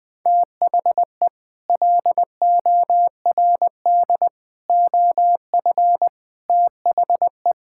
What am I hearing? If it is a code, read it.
Morse code, heard as a keyed tone, about 19 wpm, 700 Hz. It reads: THE LORD OF THE
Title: Sets of 9 Words - 20wpm